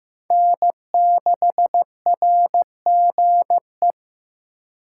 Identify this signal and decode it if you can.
Morse code, heard as a keyed tone, about 15 wpm, 700 Hz. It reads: N6RGE
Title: USA Call Signs - 5 Characters - 6x Character Spacing - 15wpm